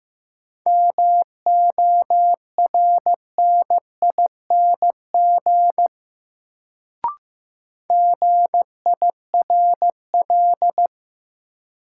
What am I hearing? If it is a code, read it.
Morse code, heard as a keyed tone, about 15 wpm, 700 Hz. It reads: MORNING GIRL
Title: Sentences from Top 400 Words - New Word Drill - 15wpm